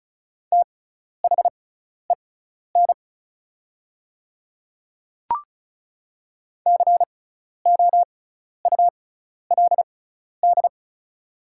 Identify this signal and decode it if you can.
Morse code, heard as a keyed tone, about 34 wpm, spaced as for 12 wpm, 700 Hz. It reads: THEN COULD
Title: 6x Character Spacing - Top 200 Words - 35wpm